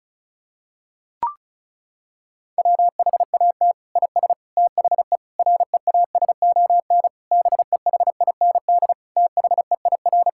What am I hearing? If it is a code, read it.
Morse code, heard as a keyed tone, about 36 wpm, 700 Hz. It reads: WHAT IS THE REASON BEHIND THEIR
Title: Sentences from Top 1000 Words - Review Part 5 - 35wpm